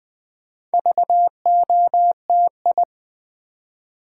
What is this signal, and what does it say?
Morse code, heard as a keyed tone, about 20 wpm, 690 Hz. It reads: VOTI
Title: Build A Word - 2 Letters At A Time - 6-Letter Words - 20wpm